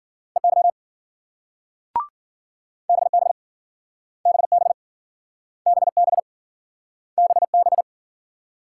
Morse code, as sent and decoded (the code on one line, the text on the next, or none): . -.-- / -... -... / -... -... / -... -... / -... -...
EY BB BB BB BB